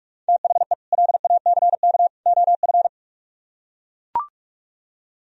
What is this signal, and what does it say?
Morse code, heard as a keyed tone, about 44 wpm, 700 Hz. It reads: THE LACK OF